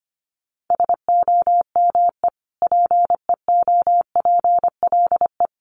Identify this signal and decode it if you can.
Morse code, heard as a keyed tone, about 25 wpm, 700 Hz. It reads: SOME PEOPLE
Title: Sentences from Top 1500 Words 25wpm